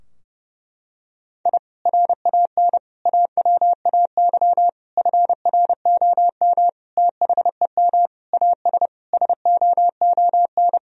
Morse code, heard as a keyed tone, about 30 wpm, 700 Hz. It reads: I RAN AWAY FROM THEM AS SOON
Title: Sentences from Top 1000 Words - Review Part 3 - 30wpm